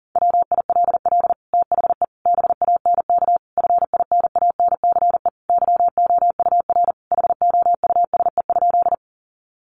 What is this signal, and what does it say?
Morse code, heard as a keyed tone, about 40 wpm, 700 Hz. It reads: WILL THE BANK FINANCE YOUR HOUSE?
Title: Sentences from Top 1500 Words 40wpm